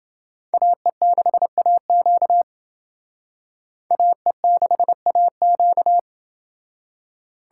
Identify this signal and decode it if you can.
Morse code, heard as a keyed tone, about 28 wpm, 700 Hz. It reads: AE6AQ AE6AQ